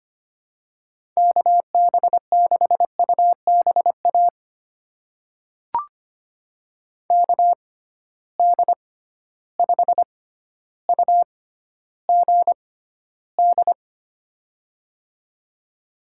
Morse code, heard as a keyed tone, about 25 wpm, spaced as for 13 wpm, 700 Hz. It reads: KB6UBA KD5UGD
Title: USA Call Signs - 6x Character Spacing - 25wpm